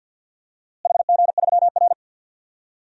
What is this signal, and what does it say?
Morse code, heard as a keyed tone, about 46 wpm, 700 Hz. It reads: HG3R
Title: DXCC-100 Call Signs - 4 Characters - Repeated 2x - 50wpm